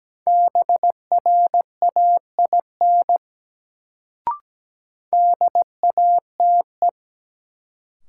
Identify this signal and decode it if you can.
Morse code, heard as a keyed tone, about 17 wpm, 700 Hz. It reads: BRAIN DATE
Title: Sentences from Top 1100 Words - New Word Drill Alt - 17wpm